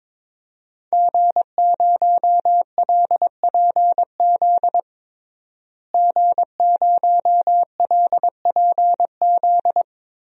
Morse code, heard as a keyed tone, about 22 wpm, 700 Hz. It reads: G0LPZ G0LPZ
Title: DXCC-100 Call Signs - 5 Characters - Repeated 3x - 22wpm